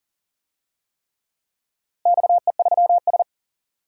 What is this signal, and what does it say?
Morse code, heard as a keyed tone, about 40 wpm, 700 Hz. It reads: XE3S
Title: DXCC-100 Call Signs - 4 Characters - Repeated 3x - 40wpm